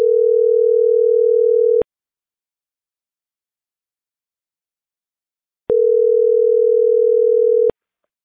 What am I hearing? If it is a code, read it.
Ringback tone (North America), 440 + 480 Hz.